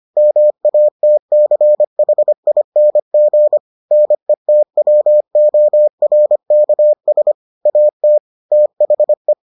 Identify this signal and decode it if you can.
Morse code, heard as a keyed tone, about 25 wpm, 600 Hz. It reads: MATCHING NETWORKS AT THE